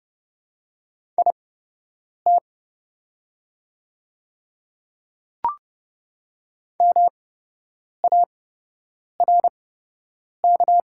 Morse code, heard as a keyed tone, about 30 wpm, spaced as for 8 wpm, 700 Hz. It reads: IT MARK
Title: 8x Character Spacing - Top 300 Words - 30wpm